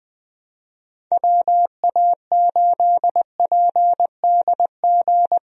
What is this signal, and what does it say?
Morse code, heard as a keyed tone, about 21 wpm, 700 Hz. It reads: WA8PDG
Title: USA Call Signs - 6 Characters - Repeated 3x - 20wpm